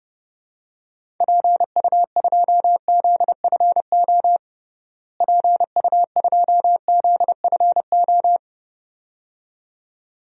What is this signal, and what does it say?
Morse code, heard as a keyed tone, about 30 wpm, 700 Hz. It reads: PU2ZFO PU2ZFO
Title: DXCC-100 Call Signs - 6 Characters - Repeated 2x - 30wpm